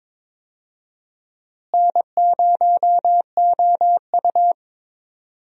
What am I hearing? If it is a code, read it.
Morse code, heard as a keyed tone, about 22 wpm, 700 Hz. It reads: N0OU
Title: USA Call Signs - 4 Characters - Repeated 3x - 22wpm